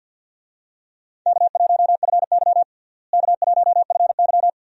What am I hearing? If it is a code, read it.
Morse code, heard as a keyed tone, about 50 wpm, 700 Hz. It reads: K1FY K1FY
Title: USA Call Signs - 4 Characters - Repeated 2x - 50wpm